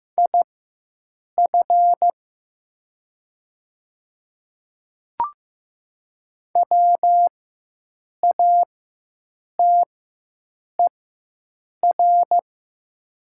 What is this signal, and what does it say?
Morse code, heard as a keyed tone, about 15 wpm, spaced as for 7 wpm, 700 Hz. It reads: IF WATER
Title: 4x Character Spacing - Top 100 Words - 15wpm